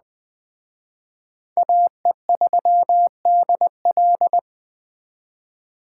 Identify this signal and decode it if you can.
Morse code, heard as a keyed tone, about 20 wpm, 700 Hz. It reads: AE3DL
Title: Speed Racing - USA 2x2 Call Signs 17wpm